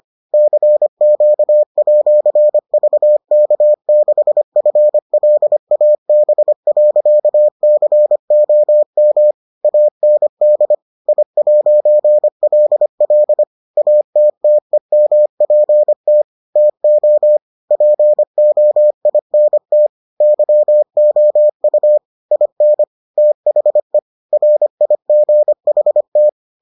Morse code, heard as a keyed tone, about 25 wpm, 600 Hz. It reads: CQ@VK6FLAB.COM AND I'LL ATTEMPT TO POINT YOU IN THE RIGHT